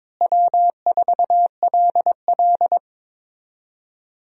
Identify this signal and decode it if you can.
Morse code, heard as a keyed tone, about 22 wpm, 700 Hz. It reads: W4LL
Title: USA Call Signs - 4 Characters - Repeated 3x - 22wpm